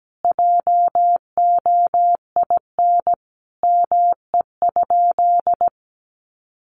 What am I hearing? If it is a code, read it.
Morse code, heard as a keyed tone, about 17 wpm, 700 Hz. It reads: JOIN ME?
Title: Sentences from Top 1400 Words 17wpm